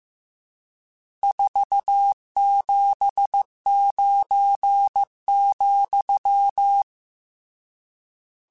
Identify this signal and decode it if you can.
Morse code, heard as a keyed tone, about 15 wpm, 800 Hz. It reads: 479,
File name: RSTs / RST-mp3s / Various RST @13